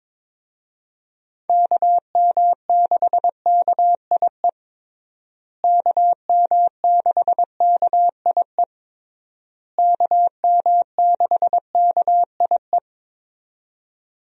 Morse code, heard as a keyed tone, about 22 wpm, 700 Hz. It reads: KM6KIE KM6KIE KM6KIE
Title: USA Call Signs - 6 Characters - Repeated 3x - 22wpm